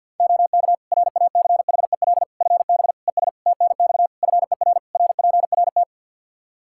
Morse code, heard as a keyed tone, about 52 wpm, 700 Hz. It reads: OK RACHEL FB ES TNX FER RPRT
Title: Practice QSO - Exchange 2 - Info - 50wpm